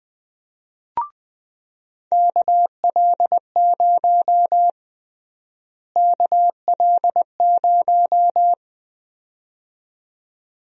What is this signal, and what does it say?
Morse code, heard as a keyed tone, about 20 wpm, 700 Hz. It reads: KL0 KL0
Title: USA Call Sign Prefixes - Repeated 2x - 20wpm